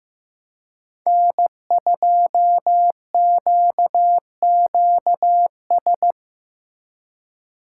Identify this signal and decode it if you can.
Morse code, heard as a keyed tone, about 15 wpm, 700 Hz. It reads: N2QQS
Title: USA Call Signs - 5 Characters - 2x Character Spacing - 15wpm